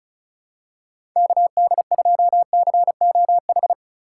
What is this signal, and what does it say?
Morse code, heard as a keyed tone, about 36 wpm, 700 Hz. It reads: KD2COH
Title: USA Call Signs - 6 Characters - 8x Character Spacing - 35wpm